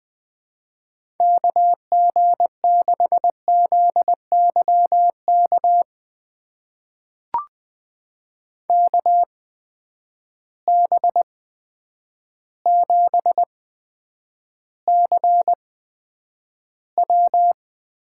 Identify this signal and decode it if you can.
Morse code, heard as a keyed tone, about 20 wpm, spaced as for 9 wpm, 700 Hz. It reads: KG6ZYK KB7CW
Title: USA Call Signs - 6 Characters - 8x Character Spacing - 20wpm